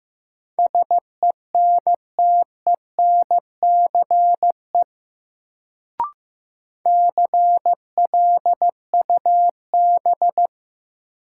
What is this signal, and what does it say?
Morse code, heard as a keyed tone, about 16 wpm, 700 Hz. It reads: SENTENCE CLUB
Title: Sentences from Top 1200 Words - New Word Drill Alt - 15wpm